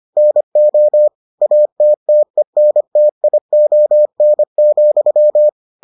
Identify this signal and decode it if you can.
Morse code, heard as a keyed tone, about 25 wpm, 600 Hz. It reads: NO ATTENTION,